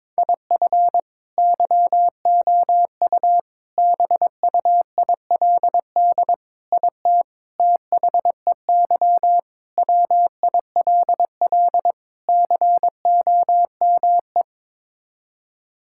Morse code, heard as a keyed tone, about 22 wpm, 700 Hz. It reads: IF YOU BUILD IT THEY WILL COME
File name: practice / 22wpm / Sets of 7 Words - 22wpm